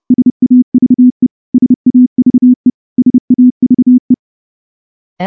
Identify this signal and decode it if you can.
Morse code, heard as a keyed tone, about 28 wpm, 250 Hz. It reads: SAVE SAVE SAVE
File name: morse-training-30WPM